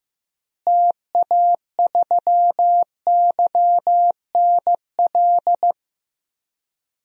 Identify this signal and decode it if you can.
Morse code, heard as a keyed tone, about 16 wpm, 700 Hz. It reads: TA3YNL
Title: DXCC-100 Call Signs - 6 Characters - Repeated 2x - 15wpm